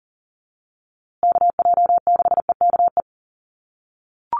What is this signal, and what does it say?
Morse code, heard as a keyed tone, about 40 wpm, 700 Hz. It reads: KJ6EKE
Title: USA Call Signs Repeated 2x 40wpm